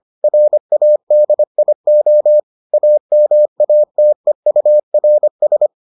Morse code, heard as a keyed tone, about 25 wpm, 600 Hz. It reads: RADIO AMATEURS